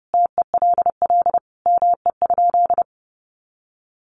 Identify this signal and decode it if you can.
Morse code, heard as a keyed tone, about 30 wpm, 700 Hz. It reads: TELL ME?
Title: Sentences from Top 600 Words 30wpm